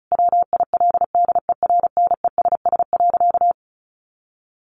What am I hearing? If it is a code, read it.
Morse code, heard as a keyed tone, about 34 wpm, 700 Hz. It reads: WILDERNESS.
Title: A Sound Of Thunder 35wpm